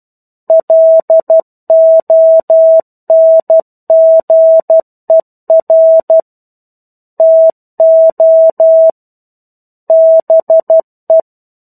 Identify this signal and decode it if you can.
Morse code, heard as a keyed tone, about 12 wpm, 650 Hz. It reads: LONGER TO BE
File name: 01-Chapter-01 12 WPM